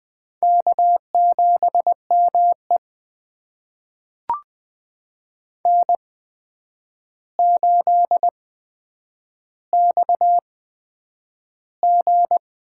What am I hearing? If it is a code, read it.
Morse code, heard as a keyed tone, about 20 wpm, spaced as for 9 wpm, 700 Hz. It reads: K7ME N8XG
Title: USA Call Signs - 4 Characters - 8x Character Spacing - 20wpm